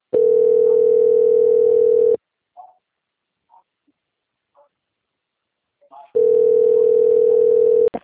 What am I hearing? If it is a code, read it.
Ringback tone (North America), 440 + 480 Hz.